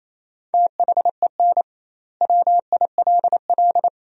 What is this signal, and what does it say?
Morse code, heard as a keyed tone, about 28 wpm, 700 Hz. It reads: THEN WILL
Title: 2x Word-Spacing - Sentences from Top 200 Words - 28wpm